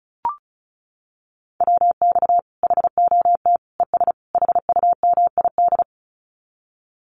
Morse code, heard as a keyed tone, about 34 wpm, 700 Hz. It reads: WX HOT ES HUMID
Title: Common QSO Phrases 35wpm